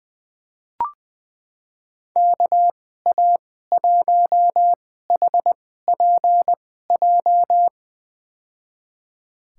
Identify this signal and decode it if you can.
Morse code, heard as a keyed tone, about 20 wpm, spaced as for 14 wpm, 700 Hz. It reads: KA1HPJ
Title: USA Call Signs - 6 Characters - 2x Character Spacing - 20wpm